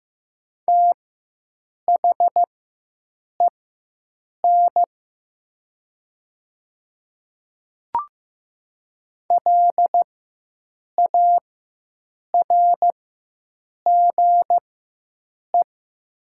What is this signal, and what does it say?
Morse code, heard as a keyed tone, about 15 wpm, spaced as for 7 wpm, 700 Hz. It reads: THEN LARGE
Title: 4x Character Spacing - Top 200 Words - 15wpm